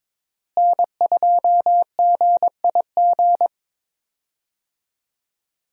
Morse code, heard as a keyed tone, about 21 wpm, 700 Hz. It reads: N2GIG
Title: USA Call Signs - 5 Characters - Repeated 3x - 22wpm